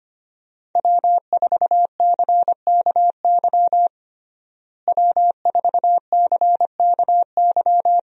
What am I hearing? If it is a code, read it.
Morse code, heard as a keyed tone, about 25 wpm, 700 Hz. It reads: W4CKY W4CKY
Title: USA Call Signs - 5 Characters - Repeated 3x - 25wpm